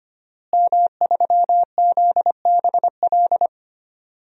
Morse code, heard as a keyed tone, about 25 wpm, 700 Hz. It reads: M3ZBL
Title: DXCC-100 Call Signs - 5 Characters - Repeated 2x - 25wpm